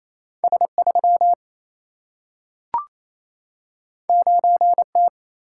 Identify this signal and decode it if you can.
Morse code, heard as a keyed tone, about 28 wpm, 700 Hz. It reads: S3 9T
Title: Single-Letter Single-Number 28wpm